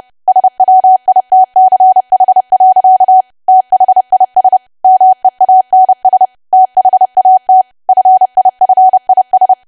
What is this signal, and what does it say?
Morse code, heard as a keyed tone, about 30 wpm, 750 Hz. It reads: SWITCH. THIS MEANS THAT FIFIS
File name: cwa / W1AW-Level3 / 30WPM